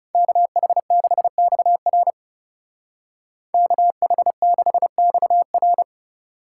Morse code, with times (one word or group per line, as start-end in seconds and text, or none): 0.1-2.1 KH6XR
3.5-5.8 KH6XR